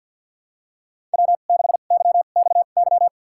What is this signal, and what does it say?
Morse code, heard as a keyed tone, about 48 wpm, spaced as for 35 wpm, 700 Hz. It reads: W6YXY
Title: USA Call Signs - 2x Character Spacing - 50wpm